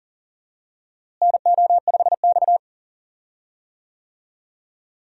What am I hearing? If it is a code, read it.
Morse code, heard as a keyed tone, about 39 wpm, 700 Hz. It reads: NO5X
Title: USA Call Signs - 4 Characters - Repeated 2x - 40wpm